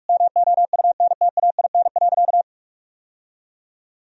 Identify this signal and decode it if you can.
Morse code, heard as a keyed tone, about 48 wpm, 700 Hz. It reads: MOUNTAIN.
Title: Sentences from Top 1000 Words - Review Part 1 - 45wpm